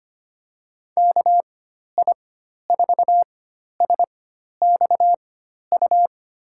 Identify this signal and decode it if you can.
Morse code, heard as a keyed tone, about 25 wpm, spaced as for 12 wpm, 700 Hz. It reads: KI4SXU
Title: USA Call Signs - 6 Characters - 4x Character Spacing - 25wpm